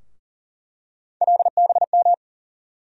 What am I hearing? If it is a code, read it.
Morse code, heard as a keyed tone, about 40 wpm, 700 Hz. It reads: LBM
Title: USA Call Sign Postfixes - 3 Letters 40wpm